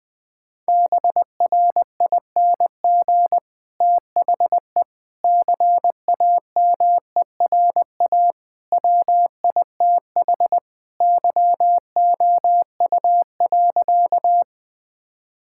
Morse code, heard as a keyed tone, about 19 wpm, 700 Hz. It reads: BRING THE CAMERA WITH YOU.
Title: Sentences from Top 1000 Words - Review Part 5 - 20wpm